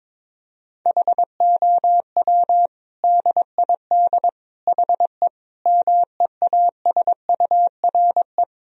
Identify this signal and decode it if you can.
Morse code, heard as a keyed tone, about 22 wpm, 700 Hz. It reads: HOW DID HE MEASURE
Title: Sentences from Top 1000 Words - Review Part 2 - 22wpm